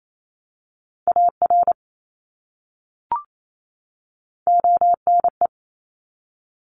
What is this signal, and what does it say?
Morse code, heard as a keyed tone, about 28 wpm, 700 Hz. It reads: AR ONE
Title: ARRL Sections 28wpm